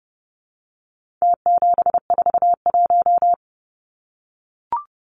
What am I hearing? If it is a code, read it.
Morse code, heard as a keyed tone, about 30 wpm, 700 Hz. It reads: T741
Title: Numbers - Contest Style - 4-Digit 30wpm